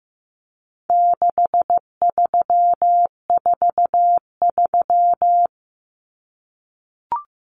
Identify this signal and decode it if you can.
Morse code, heard as a keyed tone, about 15 wpm, 700 Hz. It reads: 6343
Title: Common Cut Numbers - 4-Digit 15wpm